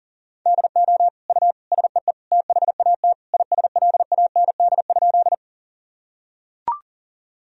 Morse code, heard as a keyed tone, about 37 wpm, 700 Hz. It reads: DO U SEE THAT ISLAND?
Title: Sentences from Top 500 Words and Abbreviations 40wpm